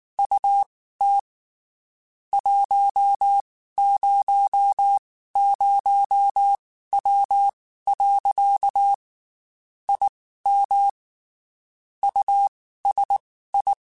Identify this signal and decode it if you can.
Morse code, heard as a keyed tone, about 19 wpm, spaced as for 12 wpm, 800 Hz. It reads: UT 100W. I MUSI
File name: QSOs / QSOs-mp3s / WD0FBT 1@18